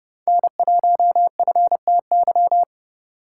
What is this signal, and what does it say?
Morse code, heard as a keyed tone, about 34 wpm, 700 Hz. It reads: N1FTY